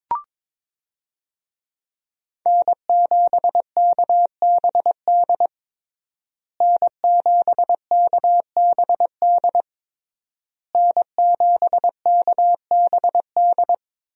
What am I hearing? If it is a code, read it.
Morse code, heard as a keyed tone, about 22 wpm, 700 Hz. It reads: N7KBD N7KBD N7KBD
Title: USA Call Signs - 5 Characters - Repeated 3x - 22wpm